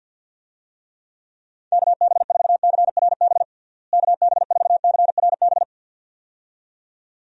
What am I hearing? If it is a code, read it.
Morse code, heard as a keyed tone, about 45 wpm, 700 Hz. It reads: KB4CRB KB4CRB
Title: USA Call Signs - 6 Characters - Repeated 2x - 50wpm